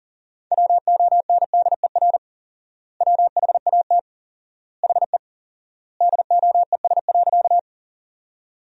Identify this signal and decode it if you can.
Morse code, heard as a keyed tone, about 40 wpm, 700 Hz. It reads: WONDER WHAT HE DOES.
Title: 4x Word-Spacing - Sentences from Top 500 Words - 40wpm